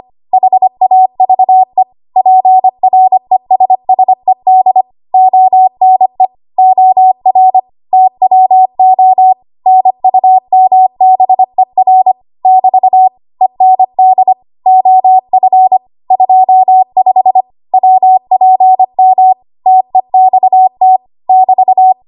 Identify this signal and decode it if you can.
Morse code, heard as a keyed tone, about 25 wpm, 750 Hz. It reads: HAVE PRESSED ONE OR TWO NUMBER <BT> END OF 25 WPM TEXT <BT>
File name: Archive / 25 WPM / 25WPM